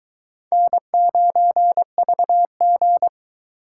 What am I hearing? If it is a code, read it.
Morse code, heard as a keyed tone, about 23 wpm, 700 Hz. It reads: N9VG